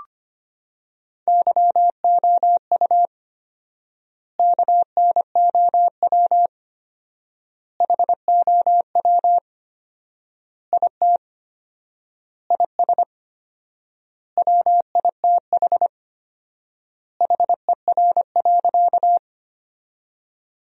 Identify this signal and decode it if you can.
Morse code, heard as a keyed tone, about 25 wpm, 700 Hz. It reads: YOU KNOW HOW IT IS WITH HER.
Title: 4x Word-Spacing - Sentences from Top 100 Words - 25wpm